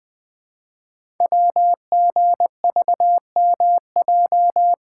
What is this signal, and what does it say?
Morse code, heard as a keyed tone, about 20 wpm, 700 Hz. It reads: WGVMJ
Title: Code Groups - 5 Letters - 20wpm